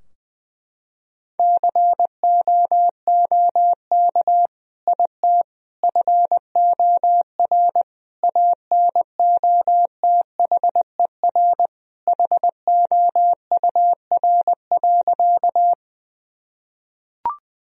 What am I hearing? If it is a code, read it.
Morse code, heard as a keyed tone, about 21 wpm, 700 Hz. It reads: COOK IT FOR ANOTHER HOUR.
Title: Sentences from Top 1000 Words - Review Part 4 - 20wpm